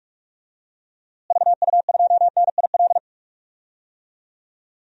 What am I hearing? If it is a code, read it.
Morse code, heard as a keyed tone, about 44 wpm, 700 Hz. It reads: VU2NIL
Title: DXCC-100 Call Signs - 6 Characters - Repeated 2x - 45wpm